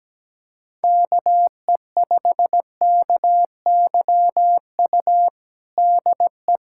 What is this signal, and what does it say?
Morse code, heard as a keyed tone, about 17 wpm, 700 Hz. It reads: KE5KYU DE
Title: USA Callsign DE Callsign 17wpm